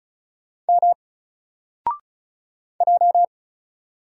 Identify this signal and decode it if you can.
Morse code, heard as a keyed tone, about 36 wpm, 700 Hz. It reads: M J
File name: Single-Letter-Number-35wpm